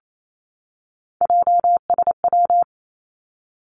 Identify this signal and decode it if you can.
Morse code, heard as a keyed tone, about 28 wpm, 700 Hz. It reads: JSW
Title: Three Challenging Letter-Number Combinations 28wpm